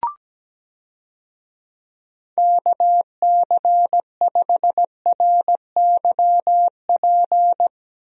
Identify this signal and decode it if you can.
Morse code, heard as a keyed tone, about 17 wpm, 700 Hz. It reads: KC5RYP
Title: USA Call Signs - 6 Characters - Repeated 2x - 17wpm